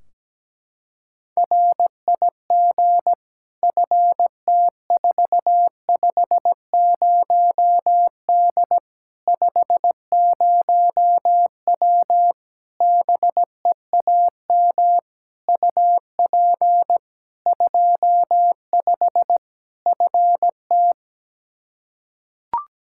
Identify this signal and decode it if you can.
Morse code, heard as a keyed tone, about 17 wpm, 700 Hz. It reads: RIG FT450D 50W BEAM UP 25 FT
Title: Practice QSO - Exchange 2 - Info - 17wpm